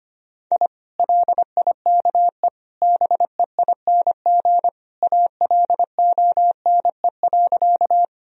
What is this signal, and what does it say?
Morse code, heard as a keyed tone, about 25 wpm, 700 Hz. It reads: I LIKE BEING ALONE.
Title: Sentences from Top 1000 Words - Review Part 3 - 25wpm